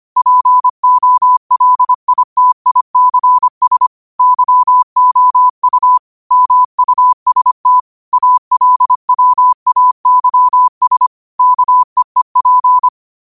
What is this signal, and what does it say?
Morse code, heard as a keyed tone, about 25 wpm, 1000 Hz. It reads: POLITICS YOU MUST ALWAYS KEEP